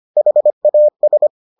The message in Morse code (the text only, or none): HAS